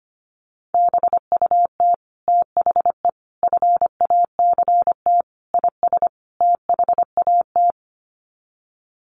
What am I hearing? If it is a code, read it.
Morse code, heard as a keyed tone, about 25 wpm, 700 Hz. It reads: BUT THE FACT IS THAT
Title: Sets of 5 Words Encore2 25wpm